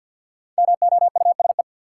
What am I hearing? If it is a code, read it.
Morse code, heard as a keyed tone, about 52 wpm, 700 Hz. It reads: MOUSE
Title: Sentences from Top 700 Words - New Word Drill - 50wpm